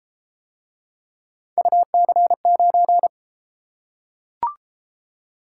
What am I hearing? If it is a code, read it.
Morse code, heard as a keyed tone, about 31 wpm, 700 Hz. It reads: UC9